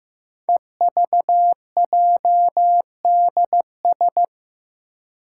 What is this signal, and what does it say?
Morse code, heard as a keyed tone, about 15 wpm, 700 Hz. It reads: EVJDS
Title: Code Groups - 5 Letters - 15wpm